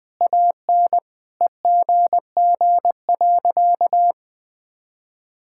Morse code, heard as a keyed tone, about 20 wpm, 700 Hz. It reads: AN EGG.
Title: Sentences with 7-Word Phrases - 20wpm